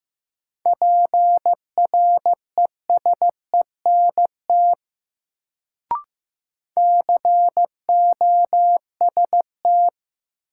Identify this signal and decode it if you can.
Morse code, heard as a keyed tone, about 15 wpm, 700 Hz. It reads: PRESENT COST
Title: Sentences from Top 800 Words - New Word Drill - 15wpm